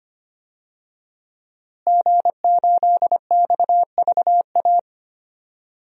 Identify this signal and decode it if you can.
Morse code, heard as a keyed tone, about 25 wpm, 700 Hz. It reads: G8XVA
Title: DXCC-100 Call Signs - 5 Characters - Repeated 2x - 25wpm